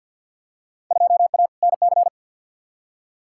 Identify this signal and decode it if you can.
Morse code, heard as a keyed tone, about 52 wpm, 700 Hz. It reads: 2A NC
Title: Field Day Exchanges - 50wpm